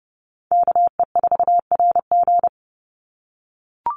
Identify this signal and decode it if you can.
Morse code, heard as a keyed tone, about 30 wpm, 700 Hz. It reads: KE4RG
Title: Call Signs - 30wpm